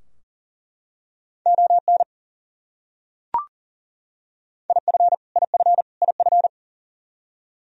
Morse code, heard as a keyed tone, about 40 wpm, 700 Hz. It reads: ON IF IF IF
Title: Top 100 Words Repeated 40wpm